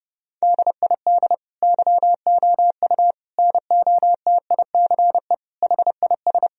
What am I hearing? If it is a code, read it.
Morse code, heard as a keyed tone, about 30 wpm, 700 Hz. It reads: DID YOU NOTICE HIS
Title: Sentences from Top 1000 Words - Review Part 3 - 30wpm